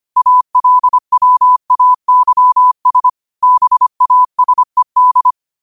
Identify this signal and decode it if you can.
Morse code, heard as a keyed tone, about 25 wpm, 1000 Hz. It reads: ALWAYS BASED
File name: qotdfiles / 25wpm